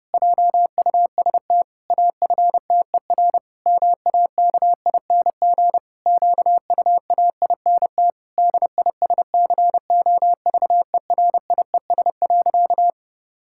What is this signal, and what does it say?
Morse code, heard as a keyed tone, about 30 wpm, 700 Hz. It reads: JUST AFTER MAKING QUAINT DISCOVERIES.
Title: Pangram Challenge v2 - 30wpm